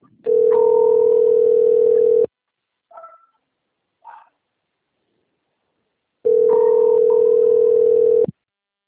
Ringback tone (North America), 440 + 480 Hz.